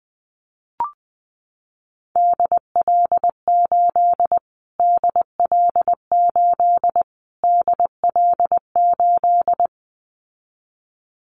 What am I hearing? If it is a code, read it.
Morse code, heard as a keyed tone, about 19 wpm, 700 Hz. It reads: DL8 DL8 DL8
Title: Callsign Prefixes for Top 25 DXCC - Repeated 3x - 20wpm